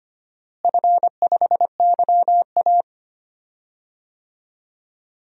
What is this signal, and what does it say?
Morse code, heard as a keyed tone, about 25 wpm, 700 Hz. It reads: F5YA
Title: DXCC-100 Call Signs - 4 Characters - Repeated 3x - 25wpm